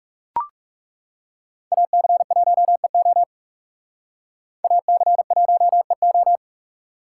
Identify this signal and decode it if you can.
Morse code, heard as a keyed tone, about 38 wpm, 700 Hz. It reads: AC1EO AC1EO